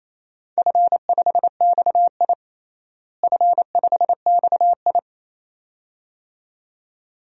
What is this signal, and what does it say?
Morse code, heard as a keyed tone, about 28 wpm, 700 Hz. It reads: F5XI F5XI
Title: DXCC-100 Call Signs - 4 Characters - Repeated 2x - 28wpm